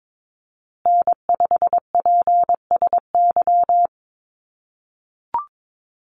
Morse code, heard as a keyed tone, about 22 wpm, 700 Hz. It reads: N5PSY
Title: USA Call Signs Repeated 2x 22wpm